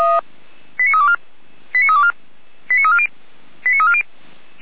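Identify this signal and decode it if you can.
Five-tone selective call (ZVEI), heard as quick runs of five tones, tones of about 70 ms. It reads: 89324 89324 89340 89340
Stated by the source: ZVEI five-tone call 89230 received by SDR, fire brigade Orlamünde (feuerwehr-orlamuende).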